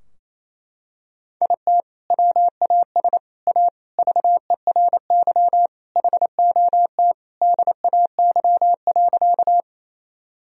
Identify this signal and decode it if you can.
Morse code, heard as a keyed tone, about 27 wpm, 700 Hz. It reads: IT WAS A VERY HOT DAY.
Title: Sentences from Top 1000 Words - Review Part 3 - 28wpm